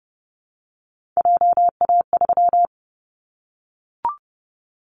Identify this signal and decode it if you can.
Morse code, heard as a keyed tone, about 30 wpm, 700 Hz. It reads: JA3
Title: Callsign Prefixes for Top 25 DXCC - Repeated 3x - 30wpm